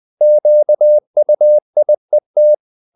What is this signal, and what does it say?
Morse code, heard as a keyed tone, about 20 wpm, 600 Hz. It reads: QUIET
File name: the old man and the sea part 2 20wpm